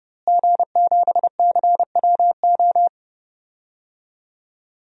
Morse code, heard as a keyed tone, about 31 wpm, 700 Hz. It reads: G7CWO
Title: DXCC-100 Call Signs - 5 Characters - Repeated 2x - 30wpm